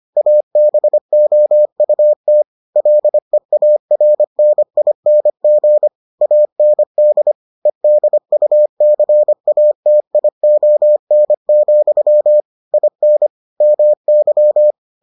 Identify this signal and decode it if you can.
Morse code, heard as a keyed tone, about 25 wpm, 600 Hz. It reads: ABOUT LEARNING AND EDUCATION, IN MY